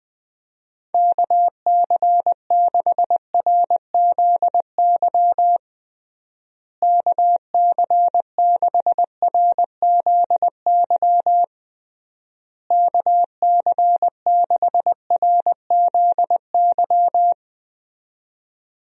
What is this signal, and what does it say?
Morse code, heard as a keyed tone, about 20 wpm, 700 Hz. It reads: KC6RZY KC6RZY KC6RZY
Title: USA Call Signs - 6 Characters - Repeated 3x - 20wpm